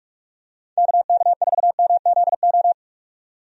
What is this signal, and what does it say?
Morse code, heard as a keyed tone, about 46 wpm, 700 Hz. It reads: KK4MZO
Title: USA Call Signs - 6 Characters - 45wpm